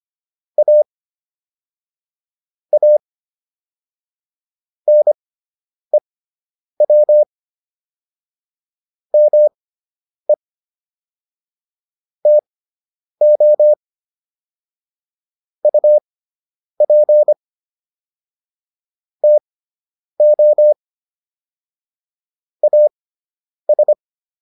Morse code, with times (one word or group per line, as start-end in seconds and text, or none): 0.6-0.8 A
2.7-3.0 A
4.9-7.2 NEW
9.1-10.3 ME
12.3-13.7 TO
15.6-17.3 UP
19.2-20.7 TO
22.6-23.9 AS